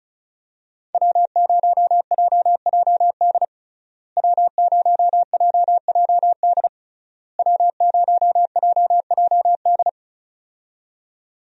Morse code, heard as a keyed tone, about 36 wpm, 700 Hz. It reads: W0JJD W0JJD W0JJD
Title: USA Call Signs - 5 Characters - Repeated 3x - 35wpm